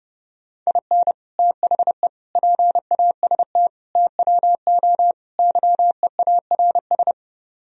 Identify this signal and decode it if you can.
Morse code, heard as a keyed tone, about 30 wpm, 700 Hz. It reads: IN THE PAST TWO YEARS